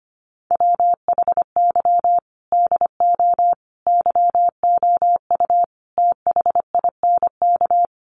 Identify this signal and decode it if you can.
Morse code, heard as a keyed tone, about 25 wpm, 700 Hz. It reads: WHY DO YOU THINK
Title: Sets of 5 Words Encore2 25wpm